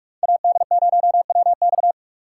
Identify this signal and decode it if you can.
Morse code, heard as a keyed tone, about 41 wpm, 700 Hz. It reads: AD0WX